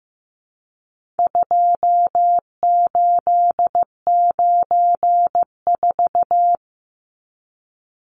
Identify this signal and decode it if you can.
Morse code, heard as a keyed tone, about 15 wpm, 700 Hz. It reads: 2894
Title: Numbers - 4-Digit 15wpm